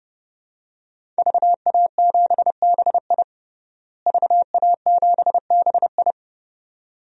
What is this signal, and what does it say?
Morse code, heard as a keyed tone, about 30 wpm, 700 Hz. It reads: VA7BI VA7BI
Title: DXCC-100 Call Signs - 5 Characters - Repeated 2x - 30wpm